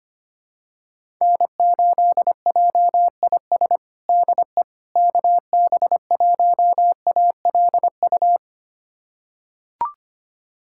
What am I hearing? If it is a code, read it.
Morse code, heard as a keyed tone, about 25 wpm, 700 Hz. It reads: N8JIS DE KB1ALU
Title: USA Callsign DE Callsign 25wpm